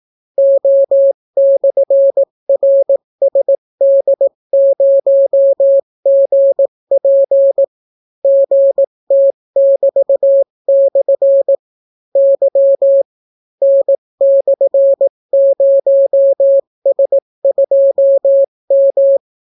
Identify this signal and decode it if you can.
Morse code, heard as a keyed tone, about 18 wpm, 550 Hz. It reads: O/RSD0GP GT<BT>/ Y N/0S2M